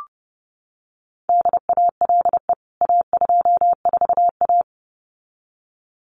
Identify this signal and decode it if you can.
Morse code, heard as a keyed tone, about 30 wpm, 700 Hz. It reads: DALE A24A
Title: CWT Exchange 30wpm